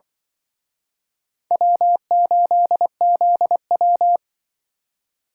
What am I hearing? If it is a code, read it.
Morse code, heard as a keyed tone, about 28 wpm, 700 Hz. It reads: W8ZW